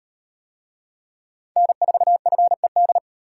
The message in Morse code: N4FED

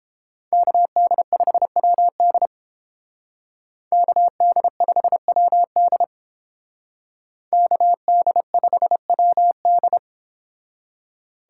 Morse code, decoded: KD5WD KD5WD KD5WD

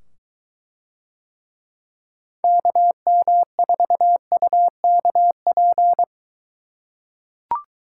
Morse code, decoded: KM4UKP

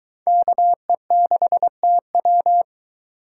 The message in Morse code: KE6TW

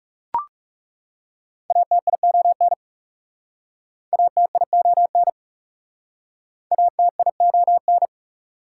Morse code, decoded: ATION ATION ATION

